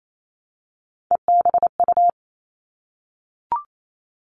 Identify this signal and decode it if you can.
Morse code, heard as a keyed tone, about 28 wpm, 700 Hz. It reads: EBU
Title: All Cut Numbers - 3-Digit 28wpm